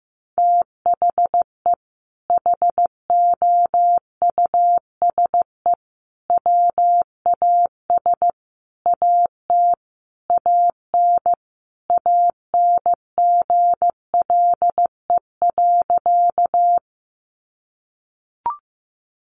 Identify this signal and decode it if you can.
Morse code, heard as a keyed tone, about 15 wpm, 700 Hz. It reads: THE HOUSE WAS AT AN ANGLE.
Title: Sentences from Top 1500 Words 15wpm